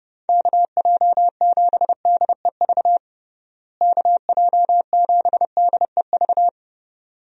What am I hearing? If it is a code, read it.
Morse code, heard as a keyed tone, about 30 wpm, 700 Hz. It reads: KJ7DEV KJ7DEV
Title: USA Call Signs - 6 Characters - Repeated 2x - 30wpm